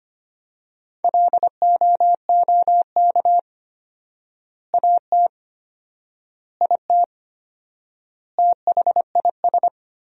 Morse code, heard as a keyed tone, about 25 wpm, 700 Hz. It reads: LOOK AT IT THIS
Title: 4x Word-Spacing - Sentences from Top 100 Words - 25wpm